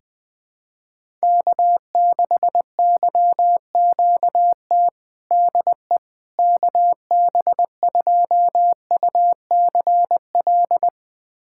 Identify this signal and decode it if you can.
Morse code, heard as a keyed tone, about 20 wpm, 700 Hz. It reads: K6YQT DE KB2UCL
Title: USA Callsign DE Callsign 20wpm